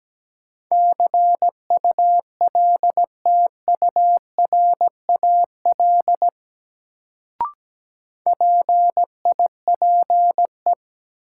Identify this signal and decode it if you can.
Morse code, heard as a keyed tone, about 17 wpm, 700 Hz. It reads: CULTURAL PIPE
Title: Sentences from Top 1500 Words - New Word Drill Alt - 17wpm